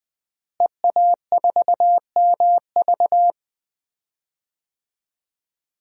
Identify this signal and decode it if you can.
Morse code, heard as a keyed tone, about 20 wpm, 700 Hz. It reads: EA4MV
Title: DXCC-100 Call Signs - 5 Characters - Repeated 3x - 20wpm